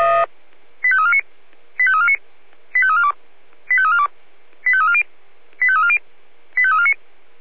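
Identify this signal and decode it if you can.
Five-tone selective call (ZVEI), heard as quick runs of five tones, tones of about 70 ms. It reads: 85349 85349 85342 85342 85340 85340 85349
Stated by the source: ZVEI five-tone call 85330 received on SDR.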